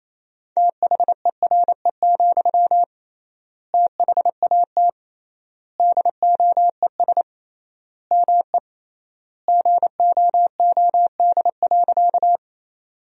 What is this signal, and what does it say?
Morse code, heard as a keyed tone, about 28 wpm, 700 Hz. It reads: THERE, THAT DOES ME GOOD.
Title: 3x Word-Spacing - Sentences from Top 200 Words - 28wpm